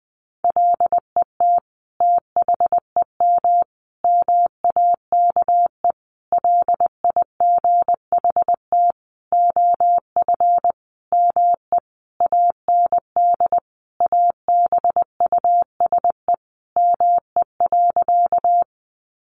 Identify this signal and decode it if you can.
Morse code, heard as a keyed tone, about 20 wpm, 700 Hz. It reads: LET THEM MAKE LIGHT OF ME AND ABUSE ME.
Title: Sentences from Top 1300 Words 20wpm